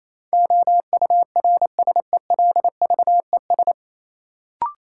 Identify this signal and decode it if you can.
Morse code, heard as a keyed tone, about 28 wpm, 700 Hz. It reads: OURSELVES
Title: Sentences from Top 1600 Words - New Word Drill - 28wpm